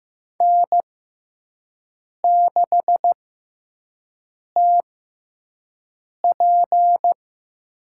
Morse code, heard as a keyed tone, about 15 wpm, 700 Hz. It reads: N 6 T P
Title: USA Call Signs - 6x Character Spacing - 15wpm